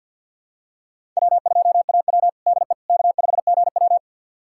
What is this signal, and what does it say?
Morse code, heard as a keyed tone, about 50 wpm, 700 Hz. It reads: W2AW DE K5GW
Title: USA Callsign DE Callsign - 4 Characters - 50wpm